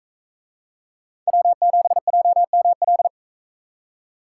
Morse code, heard as a keyed tone, about 44 wpm, 700 Hz. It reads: W7JML